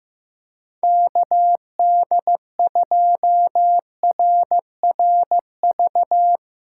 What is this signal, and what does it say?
Morse code, heard as a keyed tone, about 15 wpm, 700 Hz. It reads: KD2RRV
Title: USA Call Signs - 2x Character Spacing - 15wpm